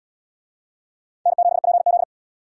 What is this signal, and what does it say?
Morse code, heard as a keyed tone, about 72 wpm, 700 Hz. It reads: N7PZ